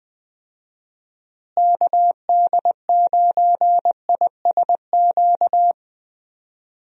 Morse code, heard as a keyed tone, about 20 wpm, 700 Hz. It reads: KD9ISQ
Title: USA Call Signs - 6 Characters - Repeated 3x - 20wpm